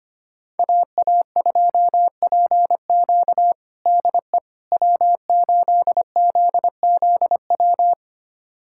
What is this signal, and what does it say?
Morse code, heard as a keyed tone, about 25 wpm, 700 Hz. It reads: AA2PQ DE W8ZZW
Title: USA Callsign DE Callsign 25wpm